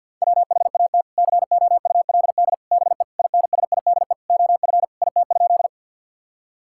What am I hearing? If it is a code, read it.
Morse code, heard as a keyed tone, about 46 wpm, 700 Hz. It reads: WHAT COULD BE INSIDE OF IT?
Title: Sentences from Top 1000 Words - Review Part 6 - 50wpm